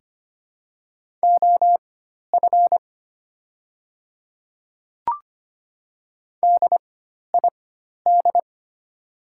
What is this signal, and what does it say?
Morse code, heard as a keyed tone, about 25 wpm, spaced as for 12 wpm, 700 Hz. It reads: OF DID
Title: 4x Character Spacing - Top 100 Words - 25wpm